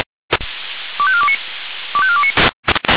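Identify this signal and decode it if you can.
Five-tone selective call (ZVEI), heard as quick runs of five tones, tones of about 70 ms. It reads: 25429 25429
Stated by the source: ZVEI five-tone selective call received by SDR, Orlamünde fire brigade (feuerwehr-orlamuende).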